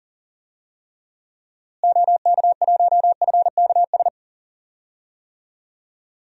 Morse code, heard as a keyed tone, about 37 wpm, 700 Hz. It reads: OK1FKS
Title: DXCC-100 Call Signs - 6 Characters - Repeated 2x - 40wpm